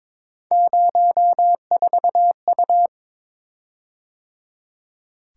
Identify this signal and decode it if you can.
Morse code, heard as a keyed tone, about 22 wpm, 700 Hz. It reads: 04U
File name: practice / 22wpm / Three Letter-Number Combinations - 22wpm